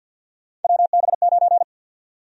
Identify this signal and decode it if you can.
Morse code, heard as a keyed tone, about 48 wpm, 700 Hz. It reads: WB9
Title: USA Call Sign Prefixes - 50wpm